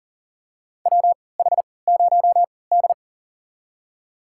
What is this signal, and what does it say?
Morse code, heard as a keyed tone, about 40 wpm, spaced as for 23 wpm, 700 Hz. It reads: WH0D
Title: USA Call Signs - 4 Characters - 3x Character Spacing - 40wpm